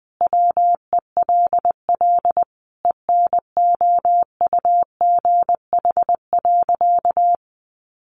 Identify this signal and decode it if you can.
Morse code, heard as a keyed tone, about 20 wpm, 700 Hz. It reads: WELL ENOUGH.
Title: Sentences from Top 600 Words 20wpm